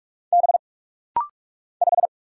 Morse code, heard as a keyed tone, about 45 wpm, 700 Hz. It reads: B 5
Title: Single Letter-Number Alt - Rapid-Fire 45wpm